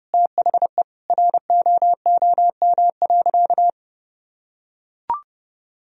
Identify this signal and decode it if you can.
Morse code, heard as a keyed tone, about 30 wpm, 700 Hz. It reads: THE ROOM.
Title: Sentences from Top 1000 Words - Review Part 1 - 30wpm